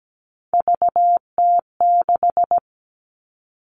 Morse code, heard as a keyed tone, about 17 wpm, 700 Hz. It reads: VT6
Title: All Cut Numbers - 3-Digit 17wpm